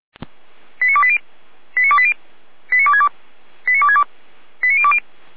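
Five-tone selective call (ZVEI), heard as quick runs of five tones, tones of about 70 ms. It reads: 89280 89280 89262 89262 89020
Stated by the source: ZVEI five-tone selective call received on SDR.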